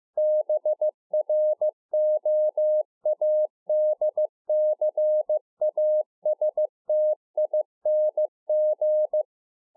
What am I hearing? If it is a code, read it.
Morse code, heard as a keyed tone, about 15 wpm, 610 Hz. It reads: BROADCASTING